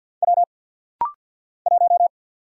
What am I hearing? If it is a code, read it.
Morse code, heard as a keyed tone, about 50 wpm, 700 Hz. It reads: W 1
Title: Single Letter-Number - Rapid-Fire 50wpm